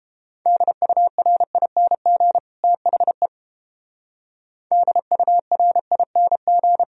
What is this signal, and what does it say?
Morse code, heard as a keyed tone, about 29 wpm, 700 Hz. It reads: DURING THE DURING